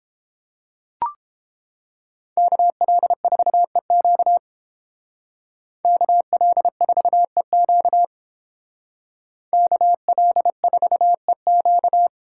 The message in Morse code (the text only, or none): KL4EQ KL4EQ KL4EQ